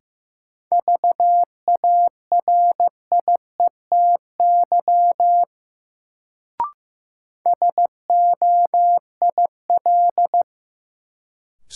Morse code, heard as a keyed tone, about 16 wpm, 700 Hz. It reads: VARIETY SOIL
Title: Sentences from Top 1000 Words - Review Part 4 - New Word Drill - 15wpm